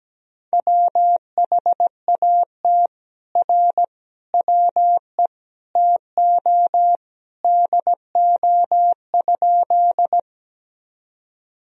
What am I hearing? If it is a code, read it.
Morse code, heard as a keyed tone, about 17 wpm, 700 Hz. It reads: WHAT R WE TO DO?
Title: Sentences from Top 100 Words and Abbreviations 17wpm